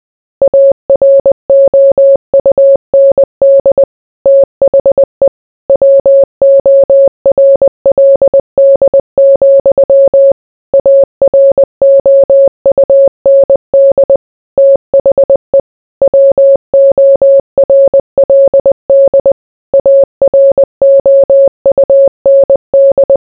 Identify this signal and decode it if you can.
Morse code, heard as a keyed tone, about 20 wpm, 550 Hz. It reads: AROUND THE WORLD, AROUND THE WORLD AROUND